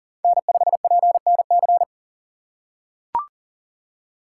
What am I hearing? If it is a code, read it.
Morse code, heard as a keyed tone, about 40 wpm, 700 Hz. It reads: N5PNC
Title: USA Call Signs - 5 Characters - 2x Character Spacing - 40wpm